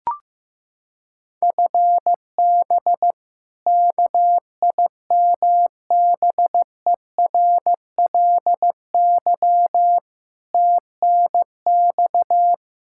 Morse code, heard as a keyed tone, about 15 wpm, 700 Hz. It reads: FB KIMBERLY TNX
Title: Practice QSO - Exchange 3 - More Info - 15wpm